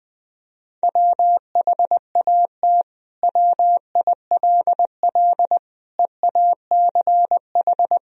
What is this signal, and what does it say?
Morse code, heard as a keyed tone, about 20 wpm, 700 Hz. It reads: WHAT WILL EACH
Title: Sentences from Top 200 Words and Abbreviations 20wpm